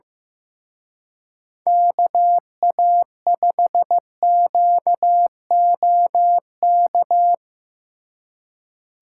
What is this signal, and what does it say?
Morse code, heard as a keyed tone, about 16 wpm, 700 Hz. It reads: KA5QOK
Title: Speed Racing - USA 2x3 Call Signs 15wpm